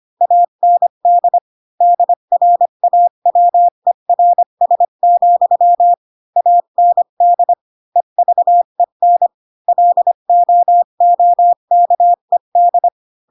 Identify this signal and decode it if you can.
Morse code, heard as a keyed tone, about 25 wpm, 700 Hz. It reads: AND DRAWERS, AND EVEN LOOKED
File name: Pinocchio / Pinocchio ch5 20 25 ik1hge